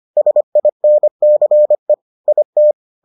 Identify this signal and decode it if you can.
Morse code, heard as a keyed tone, about 25 wpm, 600 Hz. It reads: SINCE IT